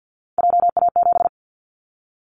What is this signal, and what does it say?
Morse code, heard as a keyed tone, about 50 wpm, 700 Hz. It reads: JA7
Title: Callsign Prefixes for Top 25 DXCC - 50wpm